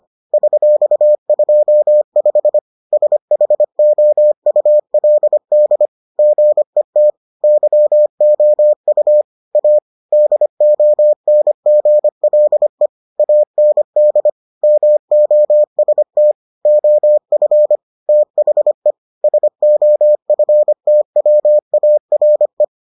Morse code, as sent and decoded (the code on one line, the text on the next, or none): ...-..- ..--- ..... / ... .... --- ..- .-.. -.. / --. - / -.-- --- ..- / .- / -.. --- -. --. .-.. / .- -. -.. / -- --- ... - / --- ..-. / - .... / ... --- ..-. - .-- .- .-. .
$25 SHOULD GET YOU A DONGLE AND MOST OF THE SOFTWARE